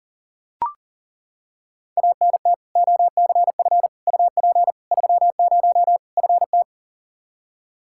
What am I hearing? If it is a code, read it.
Morse code, heard as a keyed tone, about 37 wpm, 700 Hz. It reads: ANT OCF UP 30 FT
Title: Practice QSO - Exchange 2 - Info - 40wpm